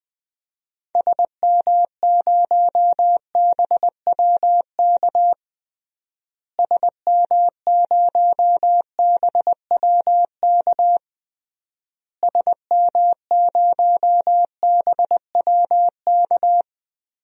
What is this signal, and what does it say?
Morse code, heard as a keyed tone, about 20 wpm, 700 Hz. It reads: SM0BWK SM0BWK SM0BWK
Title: DXCC-100 Call Signs - 6 Characters - Repeated 3x - 20wpm